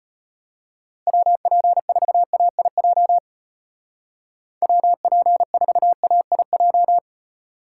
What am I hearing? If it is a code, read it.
Morse code, heard as a keyed tone, about 32 wpm, 700 Hz. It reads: WP4AIJ WP4AIJ